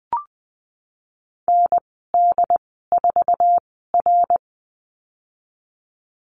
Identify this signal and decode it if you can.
Morse code, heard as a keyed tone, about 20 wpm, spaced as for 14 wpm, 700 Hz. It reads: ND4R
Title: 082 Lesson 14g - Number 2 - Call Signs 20wpm